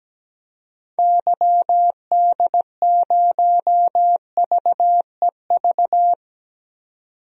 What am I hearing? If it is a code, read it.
Morse code, heard as a keyed tone, about 17 wpm, 700 Hz. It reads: YD0VEV
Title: DXCC-100 Call Signs - 6 Characters - 17wpm